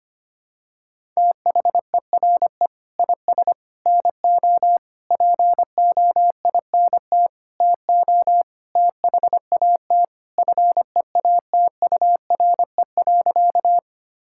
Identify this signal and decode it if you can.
Morse code, heard as a keyed tone, about 25 wpm, 700 Hz. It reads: THERE IS NO POINT TO THAT FEATURE.
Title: Sentences from Top 1000 Words - Review Part 5 - 25wpm